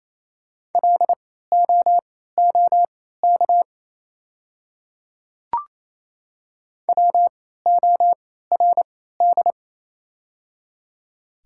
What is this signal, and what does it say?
Morse code, heard as a keyed tone, about 28 wpm, spaced as for 16 wpm, 690 Hz. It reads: LOOK WORD
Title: 3x Character Spacing - Top 100 Words - 28wpm